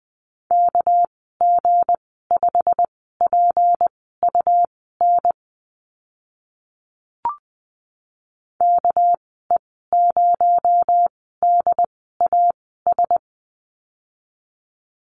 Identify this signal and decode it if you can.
Morse code, heard as a keyed tone, about 20 wpm, spaced as for 15 wpm, 700 Hz. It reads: KG5PUN KE0DAS
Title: 219 Lesson 35g - Number 0 - Call Signs 20wpm